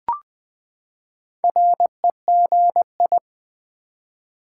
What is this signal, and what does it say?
Morse code, heard as a keyed tone, about 20 wpm, 700 Hz. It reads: REGI
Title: Build A Word - 4 Letters At A Time - 12-Letter Words - 20wpm